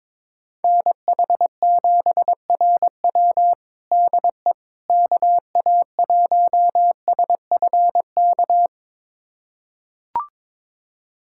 Morse code, heard as a keyed tone, about 21 wpm, 700 Hz. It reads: NH7RW DE KA1SFK
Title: USA Callsign DE Callsign 22wpm